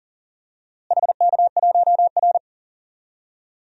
Morse code, heard as a keyed tone, about 40 wpm, 700 Hz. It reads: HK1R
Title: DXCC-100 Call Signs - 4 Characters - Repeated 3x - 40wpm